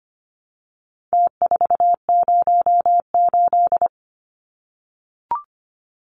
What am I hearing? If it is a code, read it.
Morse code, heard as a keyed tone, about 25 wpm, 700 Hz. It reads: T408
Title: Numbers - Contest Style - 4-Digit 25wpm